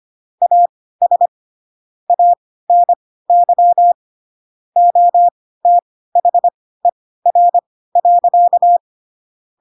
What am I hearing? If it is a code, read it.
Morse code, heard as a keyed tone, about 25 wpm, spaced as for 16 wpm, 700 Hz. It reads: AS ANY OTHER.
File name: Pinocchio / Pinocchio ch1 10 25 ik1hge